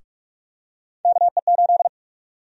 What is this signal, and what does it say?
Morse code, heard as a keyed tone, about 46 wpm, 700 Hz. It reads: KE8
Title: USA Call Sign Prefixes - 45wpm